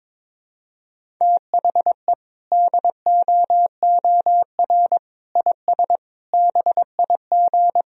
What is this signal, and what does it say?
Morse code, heard as a keyed tone, about 22 wpm, 700 Hz. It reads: THE DOOR IS BIG
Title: Sentences from Top 1000 Words - Review Part 1 - 22wpm